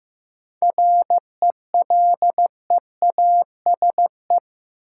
Morse code, heard as a keyed tone, about 15 wpm, 700 Hz. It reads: RELEASE
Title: Sentences from Top 1100 Words - New Word Drill Alt - 15wpm